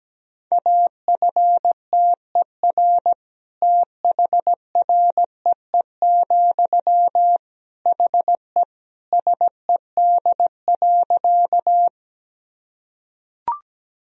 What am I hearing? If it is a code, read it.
Morse code, heard as a keyed tone, about 17 wpm, 700 Hz. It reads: AFTER THREE, HE SED.
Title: Sentences from Top 200 Words and Abbreviations 17wpm